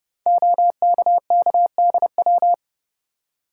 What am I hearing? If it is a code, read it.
Morse code, heard as a keyed tone, about 30 wpm, 700 Hz. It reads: OKKDW
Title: Code Groups - 5 Letter:Numbers - 30wpm